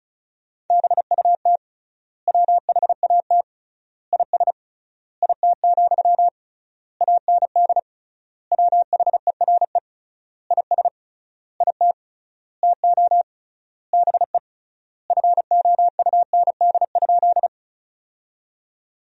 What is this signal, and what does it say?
Morse code, heard as a keyed tone, about 34 wpm, 700 Hz. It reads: BUT WHAT IS IT, AND WHERE IS IT TO BE FOUND?
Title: 3x Word-Spacing - Sentences from Top 300 Words - 35wpm